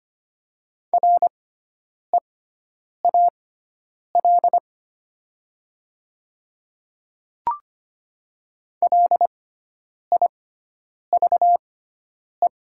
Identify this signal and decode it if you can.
Morse code, heard as a keyed tone, about 25 wpm, spaced as for 9 wpm, 700 Hz. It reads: REAL LIVE
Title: 6x Character Spacing - Top 300 Words - 25wpm